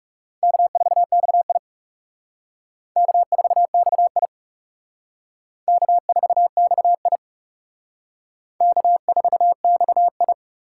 Morse code, heard as a keyed tone, about 38 wpm, 700 Hz. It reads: K4XI K4XI K4XI K4XI